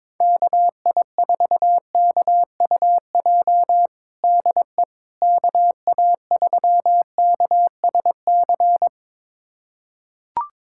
Morse code, heard as a keyed tone, about 22 wpm, 700 Hz. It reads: KI4KUJ DE KA3KSC
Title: USA Callsign DE Callsign 22wpm